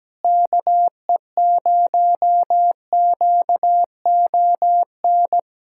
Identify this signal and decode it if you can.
Morse code, heard as a keyed tone, about 17 wpm, 700 Hz. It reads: KE0QON
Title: USA Callsign DE Callsign 17wpm